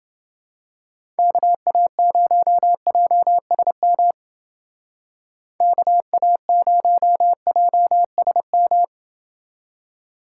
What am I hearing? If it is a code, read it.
Morse code, heard as a keyed tone, about 29 wpm, 700 Hz. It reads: KA0JSM KA0JSM